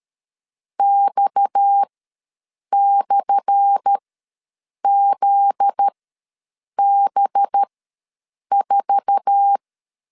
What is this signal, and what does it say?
Morse code, heard as a keyed tone, about 13 wpm, spaced as for 7 wpm, 800 Hz. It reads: X/ZB4
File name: LEZIONI / 10 SIGNS / 10A SLASH ws 12wpm 3443